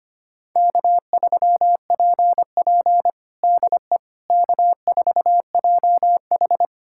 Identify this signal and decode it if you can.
Morse code, heard as a keyed tone, about 25 wpm, 700 Hz. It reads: K3PP DE K4JH
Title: USA Callsign DE Callsign - 4 Characters - 25wpm